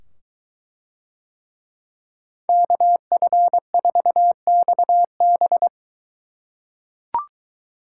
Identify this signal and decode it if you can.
Morse code, heard as a keyed tone, about 22 wpm, 700 Hz. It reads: KF4XB